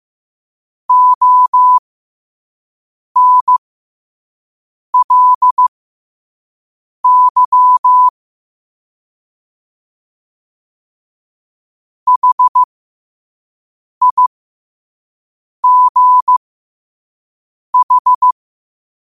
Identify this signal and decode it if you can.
Morse code, heard as a keyed tone, about 15 wpm, spaced as for 5 wpm, 1000 Hz. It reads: ONLY HIGH
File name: qotdfiles / 5wpm